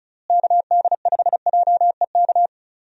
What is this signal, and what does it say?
Morse code, heard as a keyed tone, about 34 wpm, 700 Hz. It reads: KD5JEK
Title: USA Call Signs - 6 Characters - Repeated 3x - 35wpm